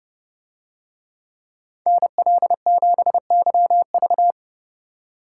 Morse code, heard as a keyed tone, about 29 wpm, 700 Hz. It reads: NL7YV